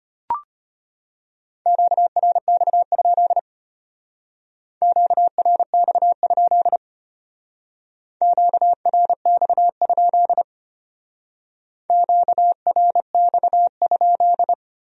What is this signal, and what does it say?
Morse code, heard as a keyed tone, about 30 wpm, 700 Hz. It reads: QRX? QRX? QRX? QRX?
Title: Speed-Racing Q-Codes - 25wpm